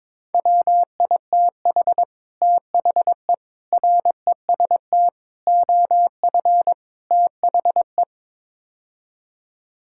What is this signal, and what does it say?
Morse code, heard as a keyed tone, about 22 wpm, 700 Hz. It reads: WITH THE REST OF THE
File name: practice / 22wpm / Build A Phrase - 5-Word Phrases - 22wpm